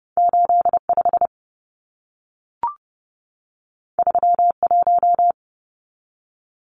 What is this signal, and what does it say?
Morse code, heard as a keyed tone, about 30 wpm, 700 Hz. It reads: 85 31
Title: Numbers - 2-Digit 30wpm